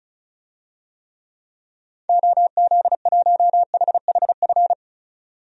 Morse code, heard as a keyed tone, about 36 wpm, 700 Hz. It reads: OZ1HHF
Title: DXCC-100 Call Signs - 6 Characters - 35wpm